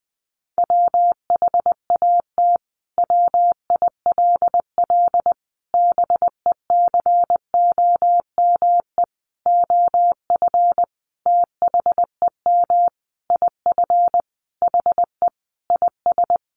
Morse code, heard as a keyed tone, about 20 wpm, 700 Hz. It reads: WHAT WILL BECOME OF THEM IF HE IS
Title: Sentences from Top 1100 Words 20wpm